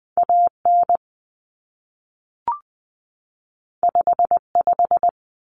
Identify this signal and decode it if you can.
Morse code, heard as a keyed tone, about 20 wpm, 700 Hz. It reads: AN 55
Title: Common Cut Numbers - 2-Digit 20wpm